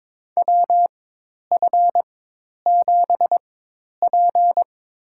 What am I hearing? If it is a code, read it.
Morse code, heard as a keyed tone, about 23 wpm, spaced as for 10 wpm, 700 Hz. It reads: WF7P
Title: USA Call Signs - 4 Characters - 4x Character Spacing - 22wpm